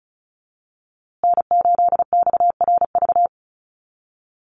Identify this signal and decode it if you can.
Morse code, heard as a keyed tone, about 34 wpm, 700 Hz. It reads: N8XRV
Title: USA Call Signs Repeated 3x 35wpm